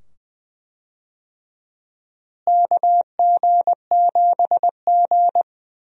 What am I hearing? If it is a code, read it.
Morse code, heard as a keyed tone, about 21 wpm, 700 Hz. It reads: KG7G
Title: USA Call Signs - 4 Characters - Repeated 2x - 20wpm